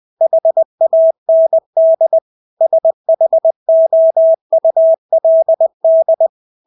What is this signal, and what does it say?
Morse code, heard as a keyed tone, about 20 wpm, 650 Hz. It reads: HAND SHOULD
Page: Engelske ord 20 wpm